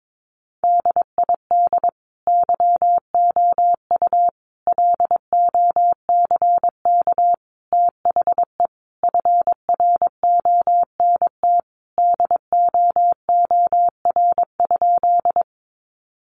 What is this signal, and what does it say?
Morse code, heard as a keyed tone, about 22 wpm, 700 Hz. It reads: DID YOU LOCK THE FRONT DOOR?
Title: Sentences from Top 1300 Words 22wpm